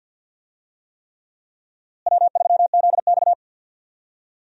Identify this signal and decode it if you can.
Morse code, heard as a keyed tone, about 51 wpm, 700 Hz. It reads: W3ZX
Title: USA Call Signs - 4 Characters - Repeated 3x - 50wpm